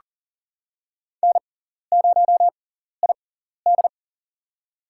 Morse code, heard as a keyed tone, about 37 wpm, spaced as for 14 wpm, 700 Hz. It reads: N0ID
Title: USA Call Signs - 4 Characters - 6x Character Spacing - 40wpm